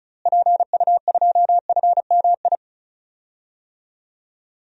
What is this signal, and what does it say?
Morse code, heard as a keyed tone, about 36 wpm, 700 Hz. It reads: PU2FMI
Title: DXCC-100 Call Signs - 6 Characters - Repeated 3x - 35wpm